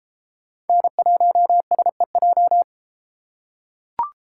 Morse code, heard as a keyed tone, about 33 wpm, 700 Hz. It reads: N1SEJ